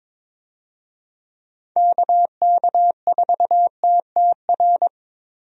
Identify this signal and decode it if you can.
Morse code, heard as a keyed tone, about 22 wpm, 700 Hz. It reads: KK4TTR
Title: USA Call Signs - 6 Characters - Repeated 2x - 22wpm